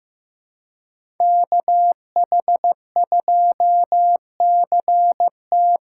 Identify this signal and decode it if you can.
Morse code, heard as a keyed tone, about 15 wpm, 700 Hz. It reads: KH2CT
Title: USA Call Signs - 5 Characters - 2x Character Spacing - 15wpm